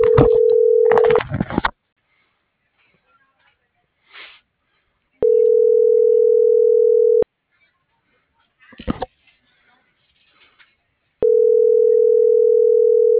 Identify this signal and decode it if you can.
Ringback tone (North America), 440 + 480 Hz.